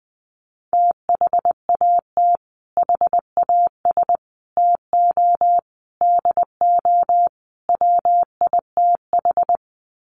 Morse code, heard as a keyed tone, about 20 wpm, 700 Hz. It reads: THAT HAS TO DO WITH
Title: Sets of 5 Words Encore2 20wpm